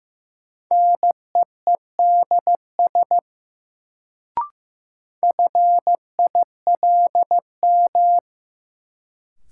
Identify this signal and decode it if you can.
Morse code, heard as a keyed tone, about 15 wpm, 700 Hz. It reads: NEEDS FILM
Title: Sentences from Top 1000 Words - Review Part 3 - New Word Drill - 15wpm